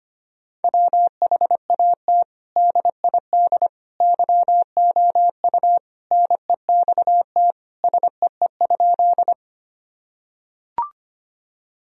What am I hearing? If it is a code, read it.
Morse code, heard as a keyed tone, about 25 wpm, 700 Hz. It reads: WHAT DID YOU NEXT SEE?
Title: Sentences from Top 1000 Words - Review Part 1 - 25wpm